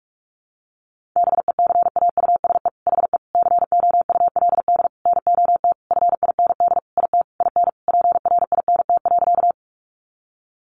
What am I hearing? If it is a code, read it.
Morse code, heard as a keyed tone, about 44 wpm, 700 Hz. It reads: BECAUSE HE COULD NOT FIND IT IN PRINT.
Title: Sentences from Top 1400 Words 45wpm